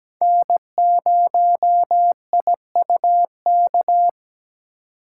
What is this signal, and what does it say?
Morse code, heard as a keyed tone, about 17 wpm, 700 Hz. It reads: N0IUK
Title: USA Call Signs - 4x Character Spacing - 17wpm